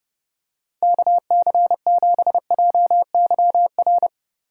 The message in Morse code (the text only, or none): KC7JYR